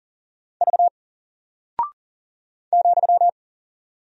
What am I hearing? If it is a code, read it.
Morse code, heard as a keyed tone, about 38 wpm, 700 Hz. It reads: V ,
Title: Single Character (Letter:Number:Punctuation:Prosign) 40wpm